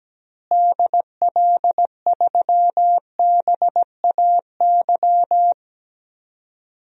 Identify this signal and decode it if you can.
Morse code, heard as a keyed tone, about 17 wpm, 700 Hz. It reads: DL3BAY
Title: DXCC-100 Call Signs - 6 Characters - Repeated 3x - 17wpm